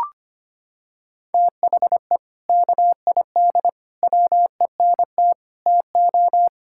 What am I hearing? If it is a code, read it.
Morse code, heard as a keyed tone, about 25 wpm, 700 Hz. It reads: THE KID WENT TO
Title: Sentences from Top 1600 Words 25wpm